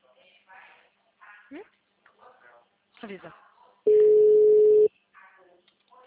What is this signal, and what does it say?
Ringback tone (Europe, 425 Hz), with speech.